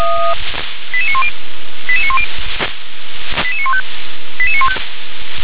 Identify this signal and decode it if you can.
Five-tone selective call (ZVEI), heard as quick runs of five tones, tones of about 70 ms. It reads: 88910 88910 88915 88915